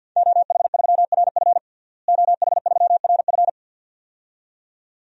Morse code, heard as a keyed tone, about 52 wpm, 700 Hz. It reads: OH3RF OH3RF
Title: DXCC-100 Call Signs - 5 Characters - Repeated 2x - 50wpm